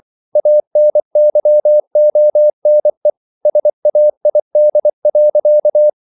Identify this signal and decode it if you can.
Morse code, heard as a keyed tone, about 24 wpm, 600 Hz. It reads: ANYONE SAID.